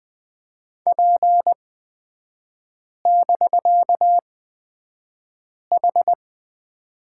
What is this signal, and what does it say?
Morse code, heard as a keyed tone, about 20 wpm, spaced as for 5 wpm, 700 Hz. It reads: P<BK>H
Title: 250 Lesson 40e - BK - Three Characters 20wpm